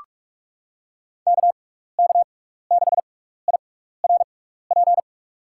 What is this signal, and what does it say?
Morse code, heard as a keyed tone, about 43 wpm, spaced as for 15 wpm, 700 Hz. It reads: KK6IRP
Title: USA Call Signs - 6 Characters - 6x Character Spacing - 45wpm